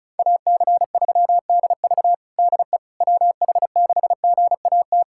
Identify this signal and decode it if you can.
Morse code, heard as a keyed tone, about 36 wpm, 700 Hz. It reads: AC3DV DE WH6GAT
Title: USA Callsign DE Callsign 35wpm